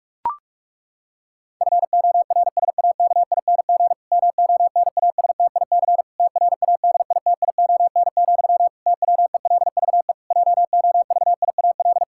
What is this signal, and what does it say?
Morse code, heard as a keyed tone, about 46 wpm, 700 Hz. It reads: FORSAKING MONASTIC TRADITION, TWELVE JOVIAL
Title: Pangram Challenge v2 - 45wpm